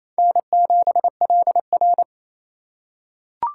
Morse code, heard as a keyed tone, about 28 wpm, 700 Hz. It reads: N7LR
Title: K1USN SST QSO Practice - 28wpm